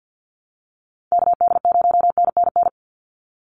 Morse code, heard as a keyed tone, about 48 wpm, 700 Hz. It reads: KD0NNN
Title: USA Call Signs 50wpm